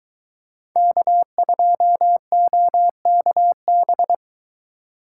Morse code, heard as a keyed tone, about 23 wpm, 700 Hz. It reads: K2OKB